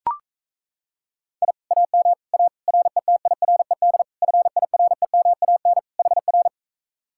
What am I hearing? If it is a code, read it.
Morse code, heard as a keyed tone, about 37 wpm, 700 Hz. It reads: I AM A RETIRED FIREMAN HR